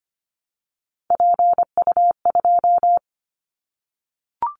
Morse code, heard as a keyed tone, about 25 wpm, 690 Hz. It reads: PU2
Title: Callsign Prefixes for Top 50 DXCC - 25wpm